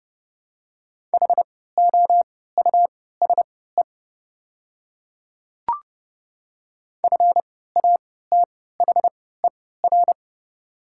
Morse code, heard as a keyed tone, about 30 wpm, spaced as for 17 wpm, 700 Hz. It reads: HOUSE FATHER
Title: 3x Character Spacing - Top 500 Words - 30wpm